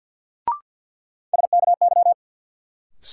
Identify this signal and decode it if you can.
Morse code, heard as a keyed tone, about 52 wpm, 700 Hz. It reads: SKY
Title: Sentences from Top 1200 Words - New Word Drill Alt - 50wpm